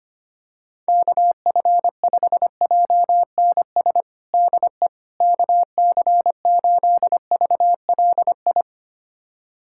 Morse code, heard as a keyed tone, about 25 wpm, 700 Hz. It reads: KF5JNS DE KC8VLI
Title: USA Callsign DE Callsign - 6 Characters - 25wpm